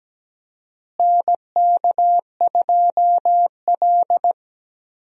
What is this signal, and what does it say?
Morse code, heard as a keyed tone, about 17 wpm, 700 Hz. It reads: NK2L
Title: USA Call Signs - 4 Characters - Repeated 3x - 17wpm